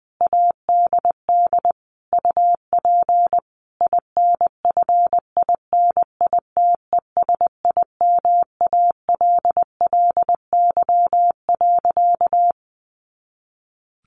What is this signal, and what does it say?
Morse code, heard as a keyed tone, about 20 wpm, 700 Hz. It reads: ADD UP INFINITESIMALLY.
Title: A Sound Of Thunder 20wpm